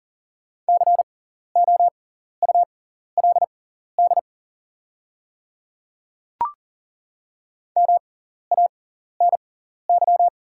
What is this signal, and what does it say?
Morse code, heard as a keyed tone, about 40 wpm, spaced as for 14 wpm, 690 Hz. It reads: COULD MANY
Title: 6x Character Spacing - Top 100 Words - 40wpm